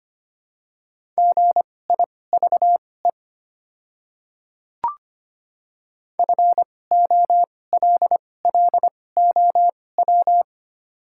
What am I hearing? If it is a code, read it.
Morse code, heard as a keyed tone, about 25 wpm, spaced as for 18 wpm, 700 Hz. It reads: GIVE FOLLOW
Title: 2x Character Spacing - Top 200 Words - 25wpm